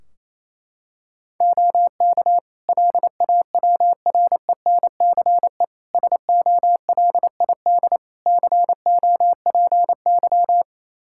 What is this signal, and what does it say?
Morse code, heard as a keyed tone, about 28 wpm, 700 Hz. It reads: OK LAWRENCE SOLID COPY
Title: Practice QSO - Exchange 3 - More Info - 28wpm